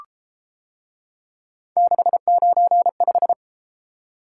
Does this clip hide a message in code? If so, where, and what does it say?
Yes: Morse code from 1.8 to 3.3 s, heard as a keyed tone, about 33 wpm, 700 Hz. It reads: 695